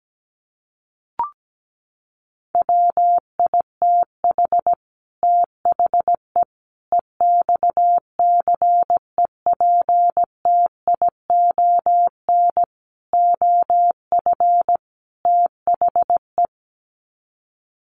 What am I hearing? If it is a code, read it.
Morse code, heard as a keyed tone, about 17 wpm, 700 Hz. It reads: WITH THE EXCEPTION OF THE
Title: Sets of 5 Words Encore 17wpm